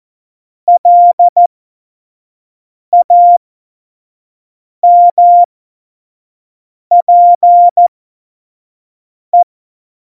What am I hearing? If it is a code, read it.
Morse code, heard as a keyed tone, about 14 wpm, spaced as for 5 wpm, 700 Hz. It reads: LAMPE
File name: Morsekurs / Witze / Witz 26. Februar 2019, 25 F70